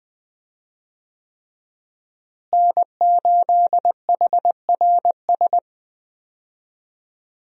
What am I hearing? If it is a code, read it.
Morse code, heard as a keyed tone, about 20 wpm, 700 Hz. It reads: N8HRS
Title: USA Call Signs - 5 Characters - Repeated 3x - 20wpm